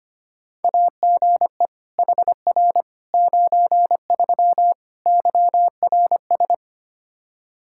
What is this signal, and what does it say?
Morse code, heard as a keyed tone, about 25 wpm, 700 Hz. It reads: AGE HR 93 YRS
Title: Practice QSO - Exchange 3 - More Info - 25wpm